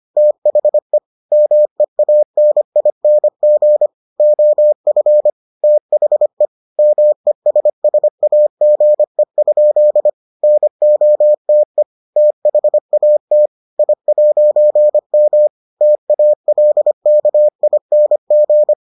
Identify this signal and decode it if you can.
Morse code, heard as a keyed tone, about 25 wpm, 600 Hz. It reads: THE MEANING OF THE MESSAGE? NOTE THAT I'M TALKING